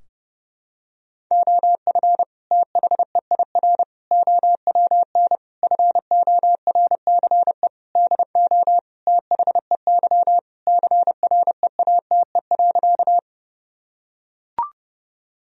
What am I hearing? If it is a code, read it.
Morse code, heard as a keyed tone, about 31 wpm, 700 Hz. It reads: OF THEIR OWN FORCE DO THEY CREATE.
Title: Sentences from Top 1000 Words - Review Part 3 - 30wpm